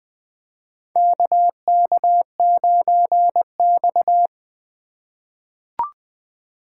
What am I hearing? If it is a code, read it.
Morse code, heard as a keyed tone, about 20 wpm, 700 Hz. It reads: KK9X
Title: USA Call Signs - 4 Characters - 3x Character Spacing - 20wpm